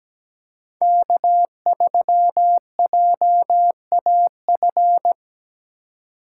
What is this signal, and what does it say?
Morse code, heard as a keyed tone, about 20 wpm, 700 Hz. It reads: K3JAF
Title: Speed Racing - USA Call Signs 17wpm